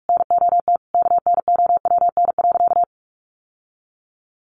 Morse code, heard as a keyed tone, about 44 wpm, 700 Hz. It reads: NOT KNOWN.
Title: Sentences from Top 1100 Words 45wpm